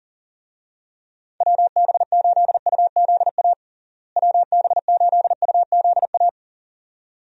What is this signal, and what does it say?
Morse code, heard as a keyed tone, about 40 wpm, 700 Hz. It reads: WB8UZA WB8UZA
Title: USA Call Signs - 6 Characters - Repeated 2x - 40wpm